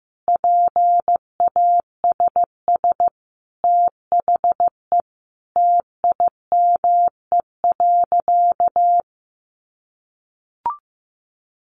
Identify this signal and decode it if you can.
Morse code, heard as a keyed tone, about 15 wpm, 700 Hz. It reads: PASS THE TIME.
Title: Sentences from Top 700 Words 15wpm